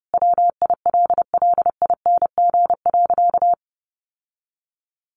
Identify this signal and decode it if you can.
Morse code, heard as a keyed tone, about 30 wpm, 700 Hz. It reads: WILLING.
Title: Sentences from Top 1400 Words 30wpm